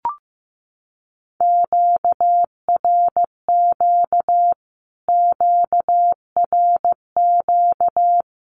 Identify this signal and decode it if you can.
Morse code, heard as a keyed tone, about 15 wpm, 700 Hz. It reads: QRQ QRQ
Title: Q-Codes Repeated 3x 15wpm